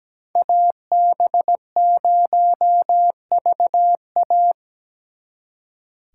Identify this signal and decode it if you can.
Morse code, heard as a keyed tone, about 20 wpm, 700 Hz. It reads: AB0VA
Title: Speed Racing - USA Call Signs 17wpm